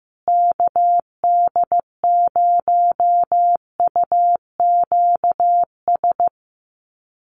Morse code, heard as a keyed tone, about 15 wpm, 700 Hz. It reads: KD0UQS
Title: USA Call Signs Repeated 3x 15wpm